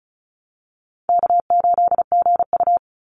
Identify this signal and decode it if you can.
Morse code, heard as a keyed tone, about 36 wpm, 700 Hz. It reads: K8GU
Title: Call Signs - 35wpm